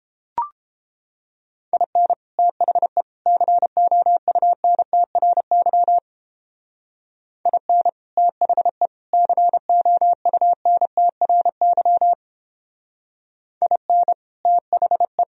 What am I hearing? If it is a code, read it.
Morse code, heard as a keyed tone, about 29 wpm, 700 Hz. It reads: IN THE COUNTRY IN THE COUNTRY IN THE